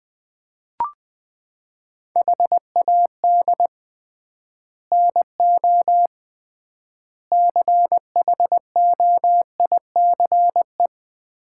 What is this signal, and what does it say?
Morse code, heard as a keyed tone, about 20 wpm, 700 Hz. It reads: HAD NO CHOICE
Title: 3x Word-Spacing - Sets of 5 Words - 20wpm